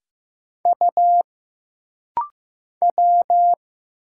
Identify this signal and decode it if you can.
Morse code, heard as a keyed tone, about 15 wpm, 700 Hz. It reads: U W
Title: Single Letter-Number Alt - Rapid-Fire 15wpm